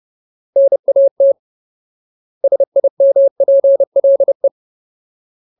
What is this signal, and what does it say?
Morse code, heard as a keyed tone, about 30 wpm, 550 Hz. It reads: NAT SIMPLE
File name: Ultimate Word List / 30WPM Ultimate